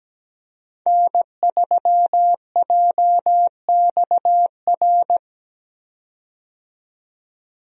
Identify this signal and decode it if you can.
Morse code, heard as a keyed tone, about 17 wpm, 700 Hz. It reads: N3JXR
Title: USA Call Signs - 5 Characters - 17wpm